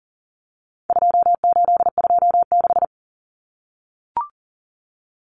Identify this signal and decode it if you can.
Morse code, heard as a keyed tone, about 40 wpm, 700 Hz. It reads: 2826
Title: Numbers - 4-Digit 40wpm